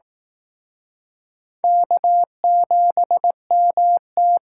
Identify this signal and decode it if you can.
Morse code, heard as a keyed tone, about 18 wpm, 700 Hz. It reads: K7MT
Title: Speed Racing - USA 1x2 Call Signs 15wpm